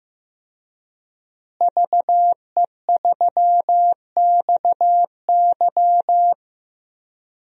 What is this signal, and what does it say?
Morse code, heard as a keyed tone, about 15 wpm, 700 Hz. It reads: VE3XY
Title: DXCC-100 Call Signs - 5 Characters - Repeated 3x - 15wpm